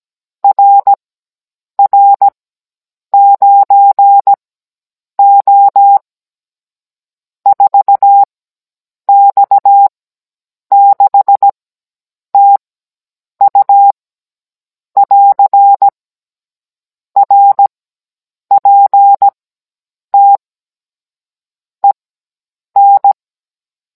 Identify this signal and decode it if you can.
Morse code, heard as a keyed tone, about 17 wpm, spaced as for 8 wpm, 800 Hz. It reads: RR9O 4X6TU <AR> RPT EN
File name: Cours-NR18-16mm